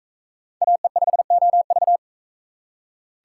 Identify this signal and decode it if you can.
Morse code, heard as a keyed tone, about 41 wpm, 700 Hz. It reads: AE5OV